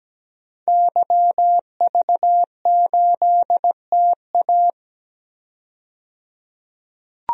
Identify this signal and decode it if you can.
Morse code, heard as a keyed tone, about 17 wpm, 700 Hz. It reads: YV8TA
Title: DXCC-100 Call Signs - 5 Characters - 17wpm